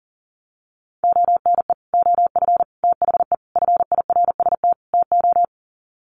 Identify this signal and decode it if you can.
Morse code, heard as a keyed tone, about 40 wpm, 700 Hz. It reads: ONE OF THE FIRST TO
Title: Sets of 5 Words Encore 40wpm